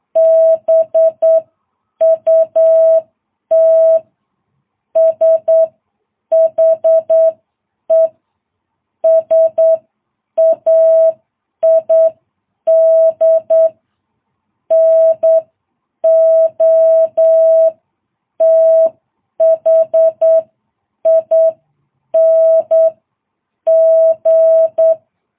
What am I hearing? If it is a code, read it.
Morse code, heard as a keyed tone, about 9 wpm, spaced as for 7 wpm, 650 Hz. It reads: BUT SHE SAID NOTHING
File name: jh8cbh